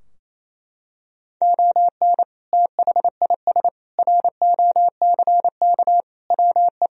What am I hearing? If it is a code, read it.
Morse code, heard as a keyed tone, about 28 wpm, 700 Hz. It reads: ON THIS ROCK WE
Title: Sentences from Top 1000 Words - Review Part 2 - 28wpm